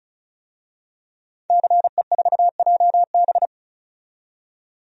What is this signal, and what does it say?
Morse code, heard as a keyed tone, about 36 wpm, 700 Hz. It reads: CE4JB
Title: DXCC-100 Call Signs - 5 Characters - 35wpm